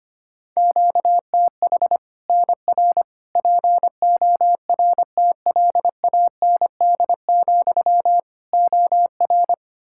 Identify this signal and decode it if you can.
Morse code, heard as a keyed tone, about 25 wpm, 700 Hz. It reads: QTH NR PORTLAND, OR
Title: Practice QSO - Exchange 1 - Beginning - 25wpm